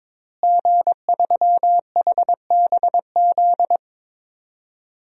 Morse code, heard as a keyed tone, about 22 wpm, 700 Hz. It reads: G3HBZ
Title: DXCC-100 Call Signs - 5 Characters - Repeated 2x - 22wpm